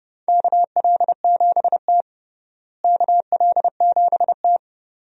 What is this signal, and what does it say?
Morse code, heard as a keyed tone, about 32 wpm, 700 Hz. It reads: KL7T KL7T
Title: USA Call Signs - 4 Characters - Repeated 2x - 30wpm